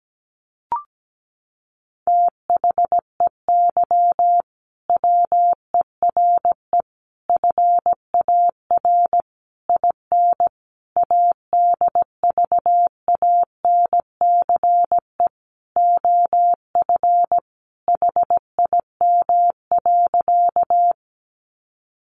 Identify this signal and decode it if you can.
Morse code, heard as a keyed tone, about 17 wpm, 700 Hz. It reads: THEY WERE FAR IN ADVANCE OF HIM.
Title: Sentences from Top 1400 Words 17wpm